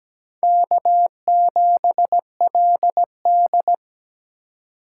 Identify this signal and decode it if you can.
Morse code, heard as a keyed tone, about 20 wpm, 700 Hz. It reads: K7LD
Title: Speed Racing - USA 1x2 Call Signs 17wpm